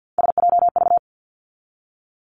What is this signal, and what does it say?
Morse code, heard as a keyed tone, about 48 wpm, 700 Hz. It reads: SJV
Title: ARRL Sections 50wpm